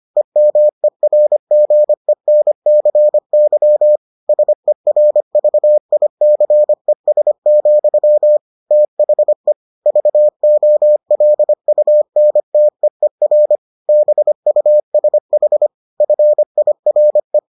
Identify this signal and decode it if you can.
Morse code, heard as a keyed tone, about 25 wpm, 600 Hz. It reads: EMERGENCY SERVICES, THE VOLUNTEER BUSH FIRE